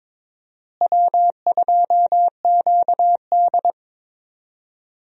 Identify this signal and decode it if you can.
Morse code, heard as a keyed tone, about 22 wpm, 700 Hz. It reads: W2QD
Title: USA Call Signs - 4 Characters - Repeated 3x - 22wpm